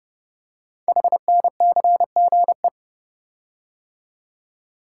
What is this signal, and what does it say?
Morse code, heard as a keyed tone, about 30 wpm, 700 Hz. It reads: HNCGE
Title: Code Groups - 5 Letters - 30wpm